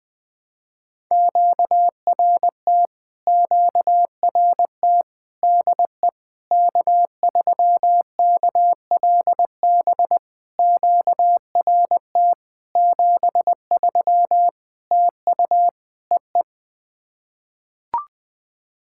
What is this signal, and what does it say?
Morse code, heard as a keyed tone, about 20 wpm, 690 Hz. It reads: QRT QRT DE K3KLB QRT 73 TU EE
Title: POTA QSO Practice - 20wpm